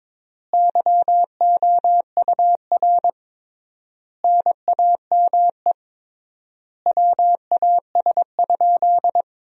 Morse code, heard as a keyed tone, about 22 wpm, 700 Hz. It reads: YOUR NAME WAS?
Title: 3x Word-Spacing - Sentences from Top 200 Words - 22wpm